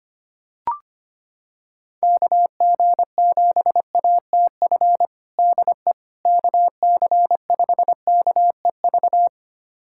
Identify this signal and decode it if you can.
Morse code, heard as a keyed tone, about 25 wpm, 700 Hz. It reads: KG7ATF DE KC5KEV
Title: USA Callsign DE Callsign - 6 Characters - 25wpm